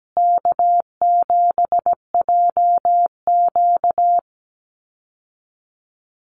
Morse code, heard as a keyed tone, about 17 wpm, 700 Hz. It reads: K7JQ
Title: Easy Call Signs Repeated 2x - 17wpm